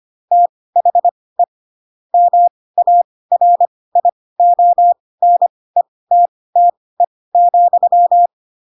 Morse code, heard as a keyed tone, about 25 wpm, spaced as for 18 wpm, 700 Hz. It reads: THE MARIONETTE,
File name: Pinocchio / Pinocchio ch7 12 25 ik1hge